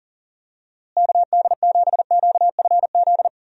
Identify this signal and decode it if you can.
Morse code, heard as a keyed tone, about 40 wpm, 700 Hz. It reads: KD7QFZ
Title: USA Call Signs - 6 Characters - Repeated 3x - 40wpm